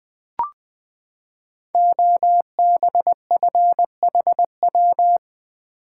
Morse code, heard as a keyed tone, about 20 wpm, 700 Hz. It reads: OBFHW
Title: Code Groups - 5 Letters - 20wpm